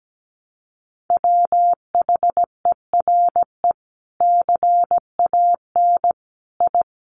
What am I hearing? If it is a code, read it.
Morse code, heard as a keyed tone, about 17 wpm, 700 Hz. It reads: WHERE CAN I
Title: Sentences from Top 1100 Words 17wpm